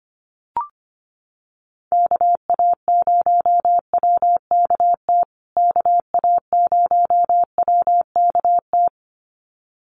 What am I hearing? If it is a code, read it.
Morse code, heard as a keyed tone, about 25 wpm, 700 Hz. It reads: KA0WKT KA0WKT
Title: USA Call Signs Repeated 2x 25wpm